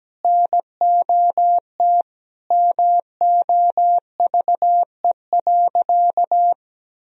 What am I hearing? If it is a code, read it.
Morse code, heard as a keyed tone, about 17 wpm, 700 Hz. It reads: NOT MOVE.
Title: Sentences from Top 1000 Words - Review Part 1 - 17wpm